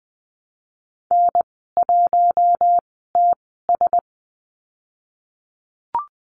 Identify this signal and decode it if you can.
Morse code, heard as a keyed tone, about 20 wpm, spaced as for 15 wpm, 700 Hz. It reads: N1TS
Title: 048 Lesson 9g - Number 4 - Call Signs 20wpm